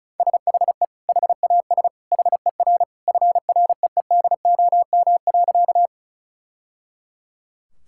Morse code, heard as a keyed tone, about 36 wpm, 700 Hz. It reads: SHE HAS HER FREEDOM.
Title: Sentences from Top 1000 Words - Review Part 5 - 35wpm